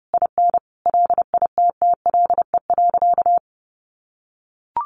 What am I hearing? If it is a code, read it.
Morse code, heard as a keyed tone, about 30 wpm, 700 Hz. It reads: IN LITTLE.
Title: Sentences from Top 1500 Words 30wpm